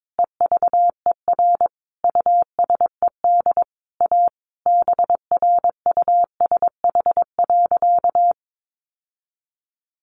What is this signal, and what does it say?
Morse code, heard as a keyed tone, about 22 wpm, 700 Hz. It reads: EVER USED A BRUSH.
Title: Sentences from Top 1100 Words 22wpm